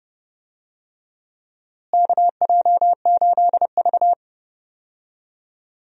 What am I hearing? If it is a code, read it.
Morse code, heard as a keyed tone, about 32 wpm, 700 Hz. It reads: KJ8V
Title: USA Call Signs - 4 Characters - Repeated 3x - 30wpm